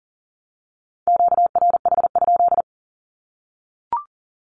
Morse code, heard as a keyed tone, about 40 wpm, 700 Hz. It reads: QRH?
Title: Q-Codes Repeated 3x 40wpm